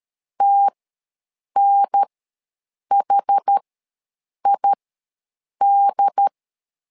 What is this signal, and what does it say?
Morse code, heard as a keyed tone, about 13 wpm, spaced as for 7 wpm, 800 Hz. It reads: TNHID
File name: LEZIONI / 2 UHDN / 2 UHDN ws 12wpm 1708